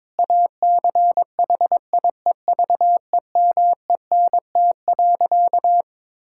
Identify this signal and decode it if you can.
Morse code, heard as a keyed tone, about 22 wpm, 700 Hz. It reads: ACHIEVEMENT.
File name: Difficult Sentences with Hints - 22wpm